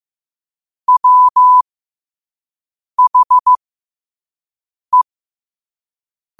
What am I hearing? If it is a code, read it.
Morse code, heard as a keyed tone, about 15 wpm, spaced as for 5 wpm, 1000 Hz. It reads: WHE
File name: qotdfiles / 5wpm